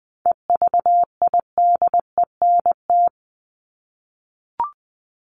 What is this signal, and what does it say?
Morse code, heard as a keyed tone, about 20 wpm, 700 Hz. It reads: EVIDENT
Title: LY Suffix - Top 100 Words with Base Words 20wpm